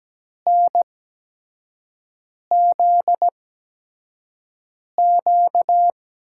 Morse code, heard as a keyed tone, about 17 wpm, spaced as for 5 wpm, 700 Hz. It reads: NZQ
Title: USA Call Signs - 5 Characters - 8x Character Spacing - 17wpm